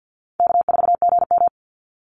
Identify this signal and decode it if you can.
Morse code, heard as a keyed tone, about 50 wpm, 700 Hz. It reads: K4GM
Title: Call Signs - 50wpm